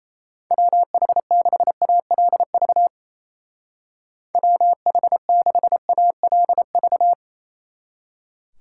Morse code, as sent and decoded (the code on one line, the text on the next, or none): .-- .... -.... .- .-.. ...- / .-- .... -.... .- .-.. ...-
WH6ALV WH6ALV